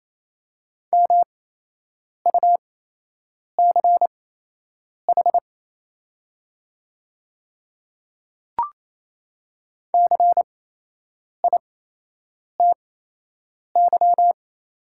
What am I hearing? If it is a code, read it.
Morse code, heard as a keyed tone, about 28 wpm, spaced as for 8 wpm, 700 Hz. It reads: MUCH CITY
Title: 8x Character Spacing - Top 300 Words - 28wpm